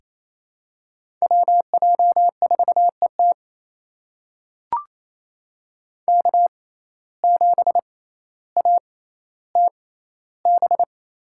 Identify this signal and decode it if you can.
Morse code, heard as a keyed tone, about 28 wpm, spaced as for 14 wpm, 700 Hz. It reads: WJ4ET K7ATB
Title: USA Call Signs - 5 Characters - 6x Character Spacing - 28wpm